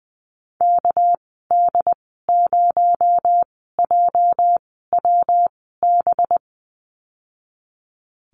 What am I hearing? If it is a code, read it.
Morse code, heard as a keyed tone, about 20 wpm, spaced as for 14 wpm, 700 Hz. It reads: KD0JWB
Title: 245 Lesson 39g - Letter z - Call Signs 20wpm